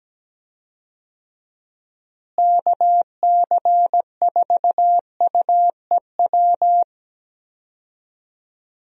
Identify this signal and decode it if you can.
Morse code, heard as a keyed tone, about 17 wpm, 700 Hz. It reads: KC4UEW
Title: USA Call Signs - 6 Characters - Repeated 2x - 17wpm